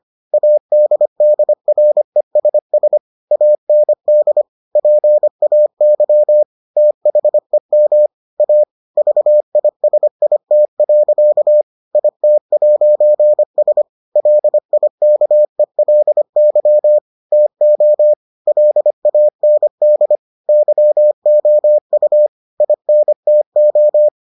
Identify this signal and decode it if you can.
Morse code, heard as a keyed tone, about 25 wpm, 600 Hz. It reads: ADDRESS AND PAY THEM A VISIT. IT'S LIKELY TO LAND YOU INTO